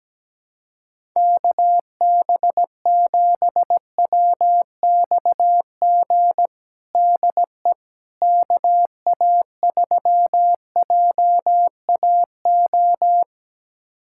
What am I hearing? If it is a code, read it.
Morse code, heard as a keyed tone, about 17 wpm, 700 Hz. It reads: KB7WXG DE KA3JAO
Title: USA Callsign DE Callsign 17wpm